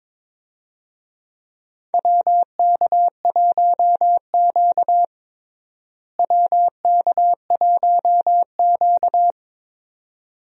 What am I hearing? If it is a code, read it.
Morse code, heard as a keyed tone, about 22 wpm, 700 Hz. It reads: WK1Q WK1Q
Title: USA Call Signs - 4 Characters - Repeated 2x - 22wpm